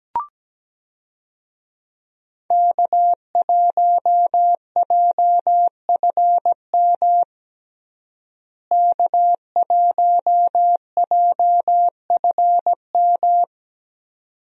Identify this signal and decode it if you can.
Morse code, heard as a keyed tone, about 17 wpm, 700 Hz. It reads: K1JFM K1JFM
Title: USA Call Signs - 5 Characters - Repeated 2x - 17wpm